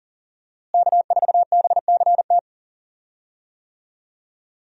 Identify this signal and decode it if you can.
Morse code, heard as a keyed tone, about 40 wpm, 700 Hz. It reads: K4BCT
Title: USA Call Signs - 5 Characters - Repeated 3x - 40wpm